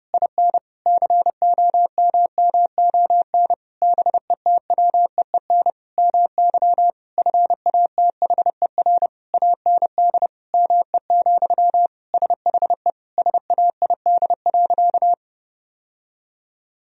Morse code, heard as a keyed tone, about 32 wpm, 700 Hz. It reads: IN COMMON BETWEEN MY FATHER AND ME, SHE SAID.
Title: Sentences from Top 1000 Words - Review Part 3 - 30wpm